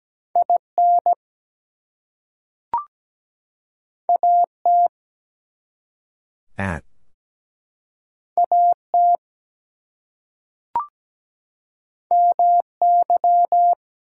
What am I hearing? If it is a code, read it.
Morse code, heard as a keyed tone, about 17 wpm, 700 Hz. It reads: IN AT AT MY
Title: Two Letter Words - 17wpm